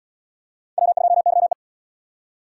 Morse code, heard as a keyed tone, about 74 wpm, 700 Hz. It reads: W2YE